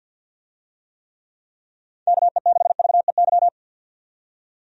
Morse code, heard as a keyed tone, about 51 wpm, 700 Hz. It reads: KE6VEY
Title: USA Call Signs - 6 Characters - 50wpm